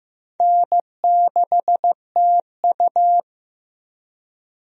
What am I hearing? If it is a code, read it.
Morse code, heard as a keyed tone, about 15 wpm, 700 Hz. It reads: N6TU
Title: USA Call Signs - 4 Characters - 15wpm